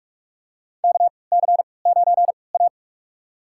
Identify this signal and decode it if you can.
Morse code, heard as a keyed tone, about 46 wpm, spaced as for 25 wpm, 700 Hz. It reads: KC9A
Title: USA Call Signs - 4 Characters - 3x Character Spacing - 45wpm